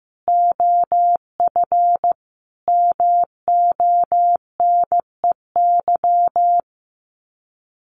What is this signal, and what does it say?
Morse code, heard as a keyed tone, about 15 wpm, 700 Hz. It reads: OF MONEY
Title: Sets of 5 Words Encore2 15wpm